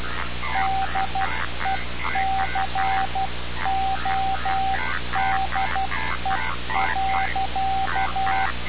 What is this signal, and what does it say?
Morse code, heard as a keyed tone, about 12 wpm, 770 Hz. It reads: DECODE.
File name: radiocap / REC z